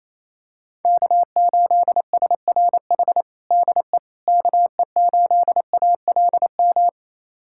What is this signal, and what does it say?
Morse code, heard as a keyed tone, about 28 wpm, 700 Hz. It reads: K8SRH DE KE8ALM
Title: USA Callsign DE Callsign 28wpm